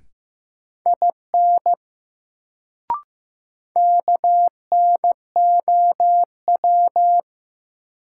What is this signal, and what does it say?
Morse code, heard as a keyed tone, about 15 wpm, 700 Hz. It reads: IN KNOW
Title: Sentences from Top 100 Words - New Word Drill - 15wpm